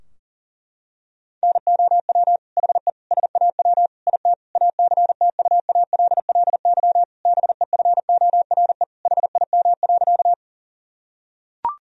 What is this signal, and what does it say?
Morse code, heard as a keyed tone, about 40 wpm, 700 Hz. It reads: NOW HE SAW IT ACTUALLY BEFORE HIM.
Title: Sentences from Top 1000 Words - Review Part 4 - 40wpm